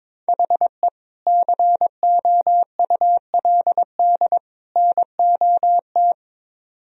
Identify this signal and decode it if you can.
Morse code, heard as a keyed tone, about 22 wpm, 700 Hz. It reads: HE COULD NOT
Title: Speed-Racing - Sets of 3 Words 22wpm